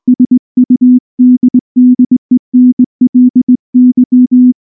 Morse code, heard as a keyed tone, about 19 wpm, 250 Hz. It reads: SUDDENLY
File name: morse / morse-training-20WPM-4